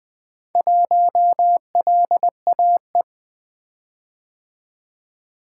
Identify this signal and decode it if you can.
Morse code, heard as a keyed tone, about 24 wpm, 700 Hz. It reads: 1LAE
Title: Speed-Racing - Four Letter-Number Combinations - 20wpm